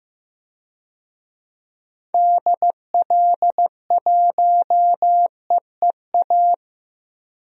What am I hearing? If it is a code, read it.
Morse code, heard as a keyed tone, about 15 wpm, 700 Hz. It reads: DL1EEA
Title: DXCC-100 Call Signs - 6 Characters - 15wpm